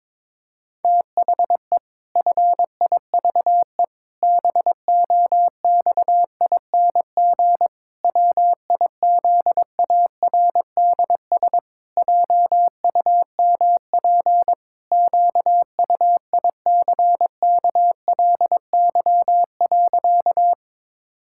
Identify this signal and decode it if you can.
Morse code, heard as a keyed tone, about 22 wpm, 700 Hz. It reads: THE FIVE BOXING WIZARDS JUMP QUICKLY.
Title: Pangram Challenge v2 - 22wpm